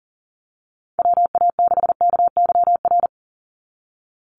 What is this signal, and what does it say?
Morse code, heard as a keyed tone, about 39 wpm, 700 Hz. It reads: WA6KYR
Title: NA SOTA Chaser Callsigns 40wpm